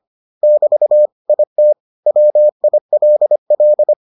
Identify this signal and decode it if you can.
Morse code, heard as a keyed tone, about 25 wpm, 600 Hz. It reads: <BT> IT WILL